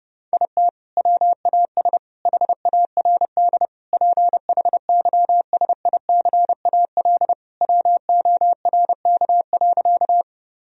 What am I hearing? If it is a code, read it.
Morse code, heard as a keyed tone, about 30 wpm, 700 Hz. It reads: IT WAS HARD PHYSICAL WORK.
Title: Sentences from Top 1000 Words - Review Part 4 - 30wpm